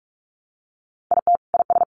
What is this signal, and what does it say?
Morse code, heard as a keyed tone, about 44 wpm, 700 Hz. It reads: IT IS
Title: Sentences from Top 700 Words 45wpm